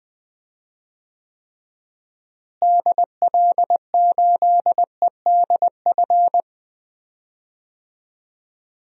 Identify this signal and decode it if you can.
Morse code, heard as a keyed tone, about 21 wpm, 700 Hz. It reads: DL8EDF
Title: DXCC-100 Call Signs - 6 Characters - Repeated 3x - 20wpm